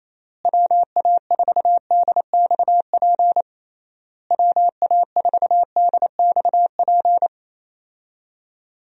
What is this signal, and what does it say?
Morse code, heard as a keyed tone, about 27 wpm, 700 Hz. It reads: WA4DXP WA4DXP
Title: USA Call Signs - 6 Characters - Repeated 3x - 28wpm